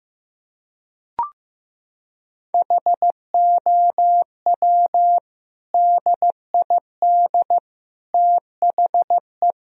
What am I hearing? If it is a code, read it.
Morse code, heard as a keyed tone, about 16 wpm, 700 Hz. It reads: HOW DID THE
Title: Sentences from Top 1000 Words - Review Part 3 - 15wpm